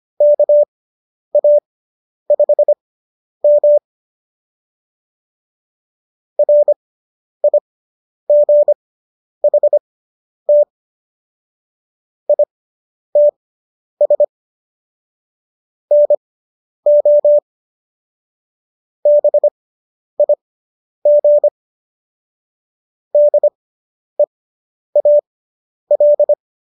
Morse code, heard as a keyed tone, about 25 wpm, spaced as for 10 wpm, 590 Hz. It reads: KA5M RIGHT ITS NO BIG DEAL